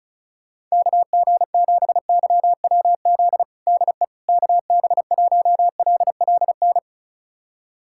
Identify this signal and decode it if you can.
Morse code, heard as a keyed tone, about 34 wpm, 700 Hz. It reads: KG7YWZ DE KB1LLN
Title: USA Callsign DE Callsign - 6 Characters - 35wpm